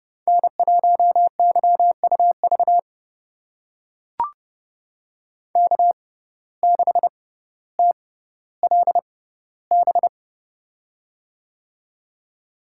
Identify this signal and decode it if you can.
Morse code, heard as a keyed tone, about 30 wpm, spaced as for 15 wpm, 700 Hz. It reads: N1YUV K6TLB
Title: USA Call Signs - 5 Characters - 6x Character Spacing - 30wpm